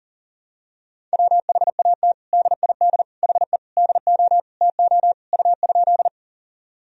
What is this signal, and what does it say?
Morse code, heard as a keyed tone, about 40 wpm, 700 Hz. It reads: WHAT DID HE DO TO U?
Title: Sentences from Top 100 Words and Abbreviations 40wpm